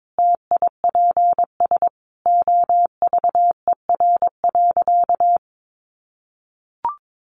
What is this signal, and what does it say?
Morse code, heard as a keyed tone, about 22 wpm, 700 Hz. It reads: TIPS OVER.
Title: Additional Easy Phrases 22wpm